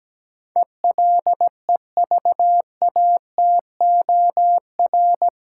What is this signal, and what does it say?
Morse code, heard as a keyed tone, about 17 wpm, 700 Hz. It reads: ELEVATOR
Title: Sentences from Top 1600 Words - New Word Drill - 17wpm